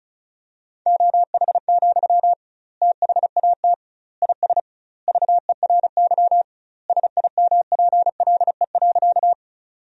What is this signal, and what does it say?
Morse code, heard as a keyed tone, about 36 wpm, 700 Hz. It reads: OH, THAT IS VERY SIMPLE.
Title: 2x Word-Spacing - Sentences from Top 500 Words - 35wpm